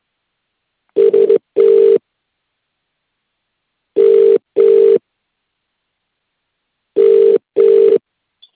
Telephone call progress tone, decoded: ringback tone